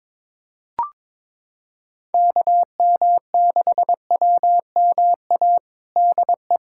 Morse code, heard as a keyed tone, about 22 wpm, 700 Hz. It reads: KM6WMA DE
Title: USA Callsign DE Callsign - 6 Characters - 22wpm